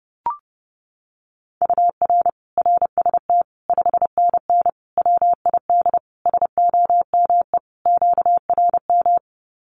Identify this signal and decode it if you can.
Morse code, heard as a keyed tone, about 30 wpm, 700 Hz. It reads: UR RST 5NN WID SOME QRM
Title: Common QSO Phrases 30wpm